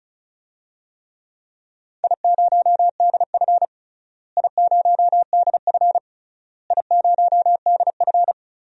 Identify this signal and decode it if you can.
Morse code, heard as a keyed tone, about 36 wpm, 700 Hz. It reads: I0DF I0DF I0DF
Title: DXCC-100 Call Signs - 4 Characters - Repeated 3x - 35wpm